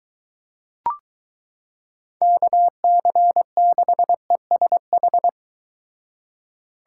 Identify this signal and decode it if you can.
Morse code, heard as a keyed tone, about 22 wpm, 700 Hz. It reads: KC6ESH